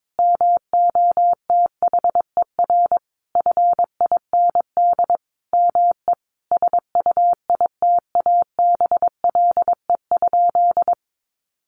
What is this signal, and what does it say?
Morse code, heard as a keyed tone, about 22 wpm, 700 Hz. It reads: MOTHER FIND ME SUITABLE?
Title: Sentences from Top 1400 Words 22wpm